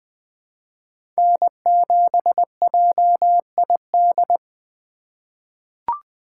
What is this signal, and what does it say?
Morse code, heard as a keyed tone, about 20 wpm, 700 Hz. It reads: N7JID
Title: USA Call Signs - 4x Character Spacing - 20wpm